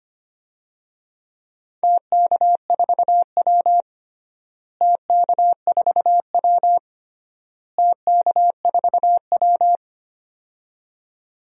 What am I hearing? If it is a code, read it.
Morse code, heard as a keyed tone, about 25 wpm, 700 Hz. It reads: TK4W TK4W TK4W
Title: DXCC-100 Call Signs - 4 Characters - Repeated 3x - 25wpm